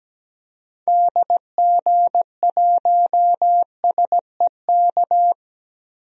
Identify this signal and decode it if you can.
Morse code, heard as a keyed tone, about 17 wpm, 700 Hz. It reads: DG1SEK
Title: DXCC-100 Call Signs - 6 Characters - 17wpm